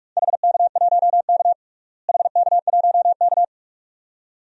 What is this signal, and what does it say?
Morse code, heard as a keyed tone, about 46 wpm, 700 Hz. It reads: HK1K HK1K
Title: DXCC-100 Call Signs - 4 Characters - Repeated 2x - 45wpm